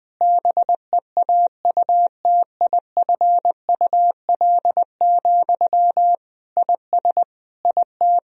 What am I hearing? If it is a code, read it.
Morse code, heard as a keyed tone, about 21 wpm, 700 Hz. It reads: BEAUTIFUL, IS IT
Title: Sentences from Top 1000 Words - Review Part 1 - 20wpm